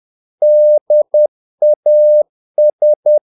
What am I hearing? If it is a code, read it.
Morse code, heard as a keyed tone, about 10 wpm, 600 Hz. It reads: DAS
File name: dk5ke / Humor-50BpM